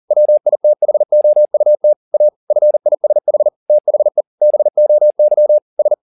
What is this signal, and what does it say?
Morse code, heard as a keyed tone, about 40 wpm, 600 Hz. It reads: WITHOUT A FISH THE BOY S
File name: the old man and the sea 1 40wpm